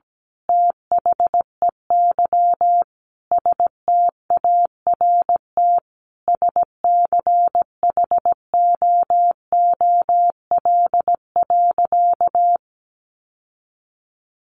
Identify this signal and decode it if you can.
Morse code, heard as a keyed tone, about 17 wpm, 700 Hz. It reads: THEY START SCHOOL.
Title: Sentences from Top 1100 Words 17wpm